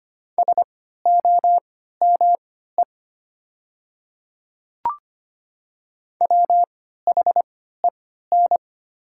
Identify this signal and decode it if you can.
Morse code, heard as a keyed tone, about 25 wpm, spaced as for 14 wpm, 700 Hz. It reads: SOME WHEN
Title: 3x Character Spacing - Top 200 Words - 25wpm